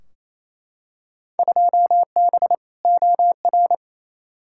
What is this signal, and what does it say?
Morse code, heard as a keyed tone, about 28 wpm, 700 Hz. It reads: 2B OR
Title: Field Day Exchanges Encore - 28wpm